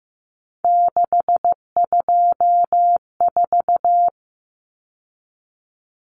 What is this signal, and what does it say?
Morse code, heard as a keyed tone, about 15 wpm, 700 Hz. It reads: 624
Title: Numbers - 3-Digit 15wpm